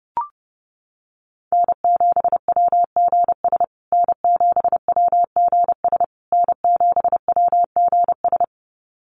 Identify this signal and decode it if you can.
Morse code, heard as a keyed tone, about 30 wpm, 700 Hz. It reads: N7WGS N7WGS N7WGS
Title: USA Call Signs Repeated 3x 30wpm